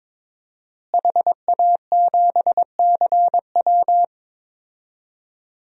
Morse code, heard as a keyed tone, about 22 wpm, 700 Hz. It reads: HA7CW
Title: DXCC-100 Call Signs - 5 Characters - Repeated 3x - 22wpm